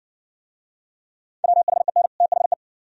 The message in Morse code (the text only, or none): WHEN THE